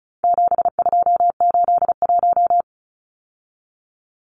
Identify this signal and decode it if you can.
Morse code, heard as a keyed tone, about 34 wpm, 700 Hz. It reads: 7281
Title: Numbers - 4-Digit 35wpm